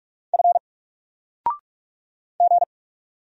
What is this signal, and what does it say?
Morse code, heard as a keyed tone, about 44 wpm, 700 Hz. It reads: F G
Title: Single Letter-Number Alt 45wpm